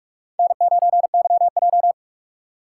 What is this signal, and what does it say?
Morse code, heard as a keyed tone, about 46 wpm, 700 Hz. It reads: N9YJ
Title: USA Call Signs - 4 Characters - 8x Character Spacing - 45wpm